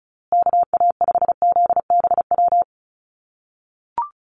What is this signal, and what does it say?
Morse code, heard as a keyed tone, about 34 wpm, 700 Hz. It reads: KA5ZBW
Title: USA Call Signs Repeated 2x 35wpm